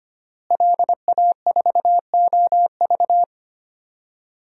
Morse code, heard as a keyed tone, about 25 wpm, 700 Hz. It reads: LA4OV
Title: DXCC-100 Call Signs - 5 Characters - Repeated 3x - 25wpm